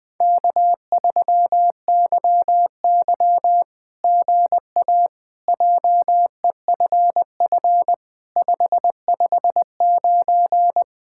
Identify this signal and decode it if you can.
Morse code, heard as a keyed tone, about 20 wpm, 700 Hz. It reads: K3YY GA JEFF 559
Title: SOTA QSO Practice 20wpm